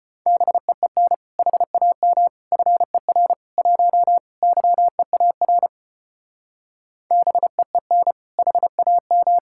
Morse code, heard as a keyed tone, about 33 wpm, 700 Hz. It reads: BEEN HAM FER 1 YEAR BEEN HAM